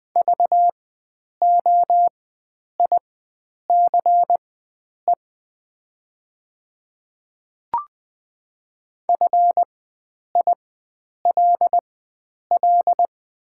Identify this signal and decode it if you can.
Morse code, heard as a keyed tone, about 21 wpm, spaced as for 9 wpm, 700 Hz. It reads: VOICE FILL
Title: 4x Character Spacing - Top 500 Words - 20wpm